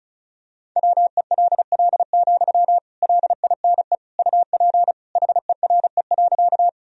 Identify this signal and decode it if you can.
Morse code, heard as a keyed tone, about 36 wpm, 700 Hz. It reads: WELL, LINE UP HERE.
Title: Sentences from Top 1000 Words - Review Part 1 - 35wpm